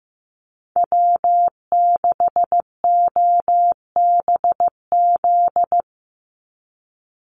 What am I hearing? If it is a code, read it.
Morse code, heard as a keyed tone, about 15 wpm, 700 Hz. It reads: W6OBZ
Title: USA Call Signs 15wpm